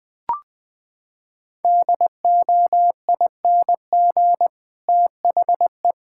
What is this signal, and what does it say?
Morse code, heard as a keyed tone, about 20 wpm, 700 Hz. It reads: DOING THE
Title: Sentences with 7-Word Phrases - 20wpm